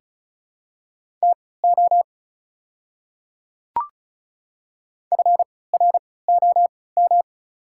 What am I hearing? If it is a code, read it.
Morse code, heard as a keyed tone, about 36 wpm, spaced as for 20 wpm, 700 Hz. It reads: TO FROM
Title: 3x Character Spacing - Top 300 Words - 35wpm